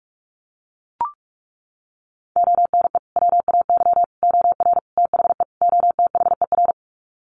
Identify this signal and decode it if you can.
Morse code, heard as a keyed tone, about 44 wpm, 700 Hz. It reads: ONE WAY OR THE OTHER
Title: Sets of 5 Words 45wpm